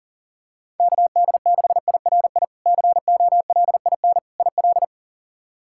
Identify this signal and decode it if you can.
Morse code, heard as a keyed tone, about 40 wpm, 700 Hz. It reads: KD6IRI COLIN IL
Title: K1USN SST QSO Practice - 40wpm